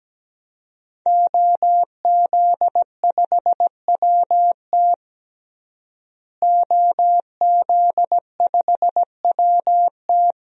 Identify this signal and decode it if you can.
Morse code, heard as a keyed tone, about 17 wpm, 700 Hz. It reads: OZ5WT OZ5WT
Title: DXCC-100 Call Signs - 5 Characters - Repeated 3x - 17wpm